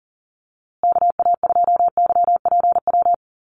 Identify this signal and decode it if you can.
Morse code, heard as a keyed tone, about 40 wpm, 700 Hz. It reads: KA2YPW
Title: USA Call Signs Repeated 3x 40wpm